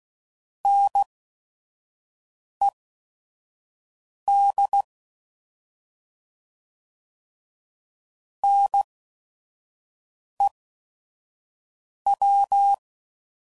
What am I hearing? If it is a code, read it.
Morse code, heard as a keyed tone, about 16 wpm, spaced as for 5 wpm, 800 Hz. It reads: NED NEW
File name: Prefixes / By First Letter / Ns